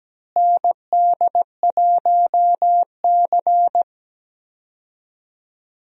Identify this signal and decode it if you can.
Morse code, heard as a keyed tone, about 17 wpm, 700 Hz. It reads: ND1C
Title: USA Call Signs - 4 Characters - Repeated 3x - 17wpm